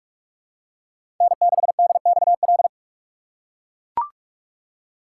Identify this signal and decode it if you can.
Morse code, heard as a keyed tone, about 42 wpm, 700 Hz. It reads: N6DXL
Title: USA Call Signs - 5 Characters - 6x Character Spacing - 45wpm